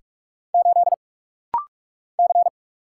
Morse code, heard as a keyed tone, about 46 wpm, 700 Hz. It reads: 8 C
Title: Single Letter-Number Alt - Rapid-Fire 45wpm